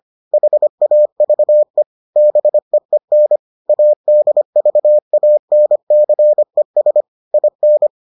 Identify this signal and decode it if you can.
Morse code, heard as a keyed tone, about 25 wpm, 600 Hz. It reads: HAVE BEEN ADVANCES IN